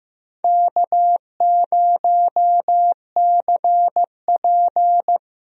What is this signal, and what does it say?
Morse code, heard as a keyed tone, about 15 wpm, 700 Hz. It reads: K0CP
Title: USA Call Signs - 4 Characters - Repeated 2x - 15wpm